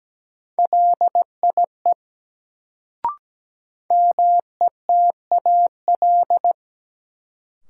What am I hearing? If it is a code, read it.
Morse code, heard as a keyed tone, about 17 wpm, 700 Hz. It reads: LIE METAL
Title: Sentences from Top 1600 Words - New Word Drill Alt - 17wpm